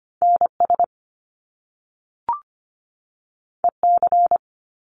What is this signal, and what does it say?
Morse code, heard as a keyed tone, about 25 wpm, 700 Hz. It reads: NS EC
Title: Top Two Letter Combinations in Words 25wpm